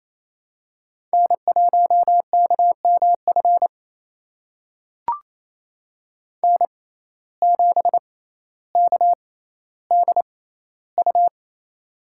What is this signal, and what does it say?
Morse code, heard as a keyed tone, about 28 wpm, spaced as for 14 wpm, 700 Hz. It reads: N1KMF N7KDU
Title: USA Call Signs - 5 Characters - 6x Character Spacing - 28wpm